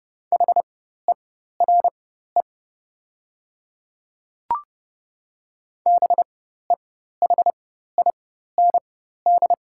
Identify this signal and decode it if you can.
Morse code, heard as a keyed tone, about 30 wpm, spaced as for 14 wpm, 700 Hz. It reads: HERE BEHIND
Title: 4x Character Spacing - Top 500 Words - 30wpm